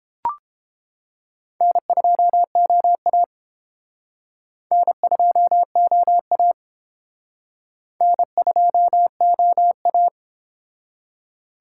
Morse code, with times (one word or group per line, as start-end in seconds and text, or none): 1.6-3.2 N2OA
4.7-6.5 N2OA
8.0-10.1 N2OA